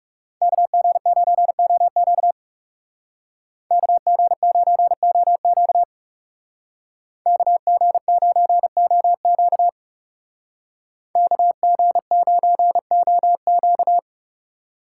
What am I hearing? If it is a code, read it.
Morse code, heard as a keyed tone, about 39 wpm, 700 Hz. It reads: KG9OQ KG9OQ KG9OQ KG9OQ